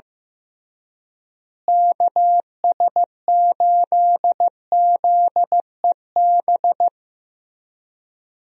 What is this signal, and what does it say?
Morse code, heard as a keyed tone, about 15 wpm, 700 Hz. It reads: KS8ZEB
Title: USA Call Signs - 6 Characters - Repeated 3x - 15wpm